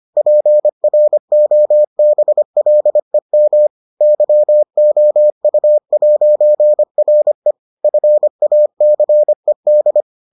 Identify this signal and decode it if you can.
Morse code, heard as a keyed tone, about 25 wpm, 600 Hz. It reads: PROBLEM YOU'RE FACED